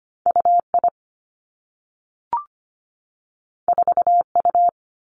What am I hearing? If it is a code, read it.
Morse code, heard as a keyed tone, about 25 wpm, 700 Hz. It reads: U I 4 U
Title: Two Challenging Letter-Number Combinations 25wpm